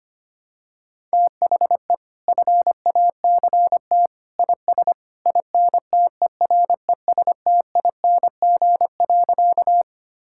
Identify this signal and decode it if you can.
Morse code, heard as a keyed tone, about 25 wpm, 700 Hz. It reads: THE FACT IS INTERESTING.
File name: practice / 25wpm / Sentences with 2-Word Phrases - 25wpm